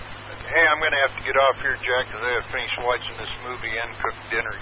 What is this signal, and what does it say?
Emergency Alert System (SAME) tones, heard as harsh buzzing bursts, with speech. It is heard from 0.8 to 3.9 s.